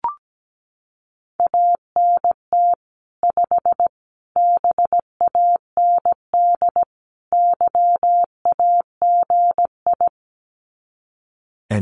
Morse code, heard as a keyed tone, about 17 wpm, 700 Hz. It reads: ANT 5 BAND YAGI
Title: Common QSO Phrases 17wpm